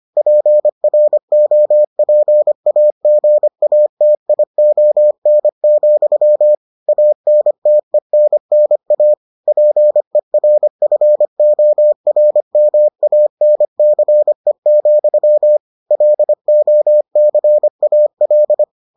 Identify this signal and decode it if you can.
Morse code, heard as a keyed tone, about 25 wpm, 600 Hz. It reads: PROPAGATION, ANTENNA PERFORMANCE, LOCAL